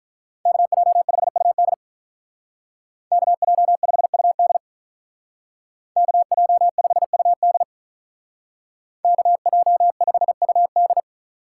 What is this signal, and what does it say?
Morse code, heard as a keyed tone, about 45 wpm, 700 Hz. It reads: KJ5UD KJ5UD KJ5UD KJ5UD